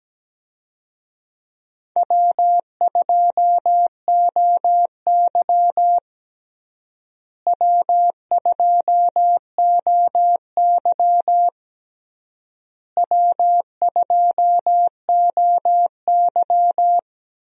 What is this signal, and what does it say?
Morse code, heard as a keyed tone, about 17 wpm, 700 Hz. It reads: W2OY W2OY W2OY
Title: USA Call Signs - 4 Characters - Repeated 3x - 17wpm